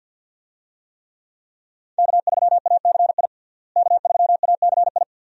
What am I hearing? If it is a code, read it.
Morse code, heard as a keyed tone, about 53 wpm, 700 Hz. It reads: K3ACI K3ACI
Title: USA Call Signs - 5 Characters - Repeated 2x - 50wpm